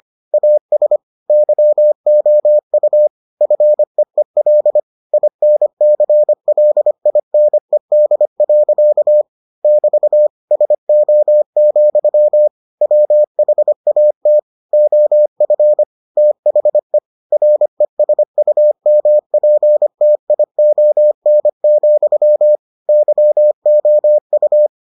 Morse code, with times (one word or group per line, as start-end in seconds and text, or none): 0.3-1.0 AS
1.3-3.1 YOU
3.4-4.8 FEEL
5.1-9.2 INCLINED.
9.6-10.3 <BT>
10.5-12.5 SO,
12.8-14.4 WHAT
14.7-15.8 OF
16.2-17.0 THE
17.3-22.6 RESUMPTION,
22.9-24.7 YOU